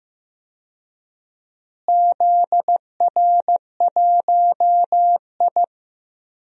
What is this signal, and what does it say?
Morse code, heard as a keyed tone, about 15 wpm, 700 Hz. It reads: ZR1I
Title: DXCC-100 Call Signs - 4 Characters - Repeated 3x - 15wpm